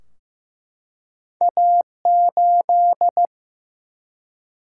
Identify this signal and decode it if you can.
Morse code, heard as a keyed tone, about 15 wpm, 700 Hz. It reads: A8
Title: Single-Letter Single-Number 15wpm